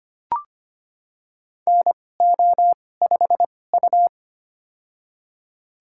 Morse code, heard as a keyed tone, about 25 wpm, spaced as for 18 wpm, 700 Hz. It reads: NO5U
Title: USA Call Signs - 4 Characters - 2x Character Spacing - 25wpm